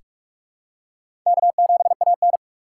Morse code, heard as a keyed tone, about 46 wpm, 700 Hz. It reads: K7AN
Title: USA Call Signs - 4 Characters - 3x Character Spacing - 45wpm